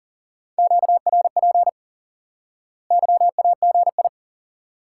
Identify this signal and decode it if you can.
Morse code, heard as a keyed tone, about 37 wpm, 700 Hz. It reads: QRP YAGI
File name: IQE Course 3 - Lesson 89 - YAGI - Two Elements - 40wpm